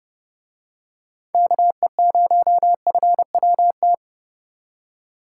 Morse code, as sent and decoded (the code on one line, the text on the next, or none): -.- . ----- ..-. .-- -
KE0FWT